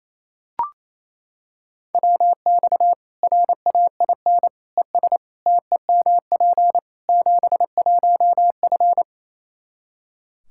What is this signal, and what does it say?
Morse code, heard as a keyed tone, about 28 wpm, 700 Hz. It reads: WX RAIN ES TEMP 71F
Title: K1USN SST QSO Practice 28wpm